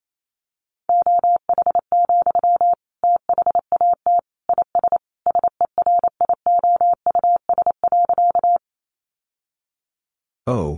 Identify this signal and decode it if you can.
Morse code, heard as a keyed tone, about 28 wpm, 700 Hz. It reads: OH, THAT IS SERIOUS.
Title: Sentences from Top 1100 Words 28wpm